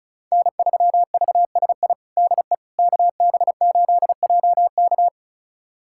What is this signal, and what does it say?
Morse code, heard as a keyed tone, about 34 wpm, 700 Hz. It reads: N3VSI DE KB8JK
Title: USA Callsign DE Callsign - 5 Characters - 35wpm